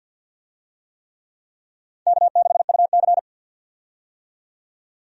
Morse code, heard as a keyed tone, about 47 wpm, 700 Hz. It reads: K6UC
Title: USA Call Signs - 4 Characters - 50wpm